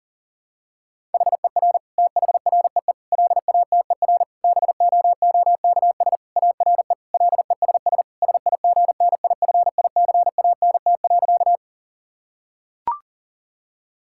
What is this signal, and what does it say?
Morse code, heard as a keyed tone, about 40 wpm, 700 Hz. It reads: HER THREE LATER BOOKS ARE LESS SIGNIFICANT.
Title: Sentences from Top 1000 Words - Review Part 6 - 40wpm